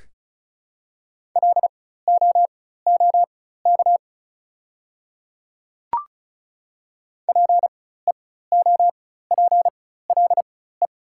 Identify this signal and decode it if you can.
Morse code, heard as a keyed tone, about 34 wpm, spaced as for 16 wpm, 700 Hz. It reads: LOOK PEOPLE
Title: 4x Character Spacing - Top 100 Words - 35wpm